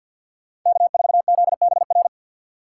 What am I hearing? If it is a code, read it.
Morse code, heard as a keyed tone, about 62 wpm, 700 Hz. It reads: K4ZBR
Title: Speed Racing - USA 1x3 Call Signs 50wpm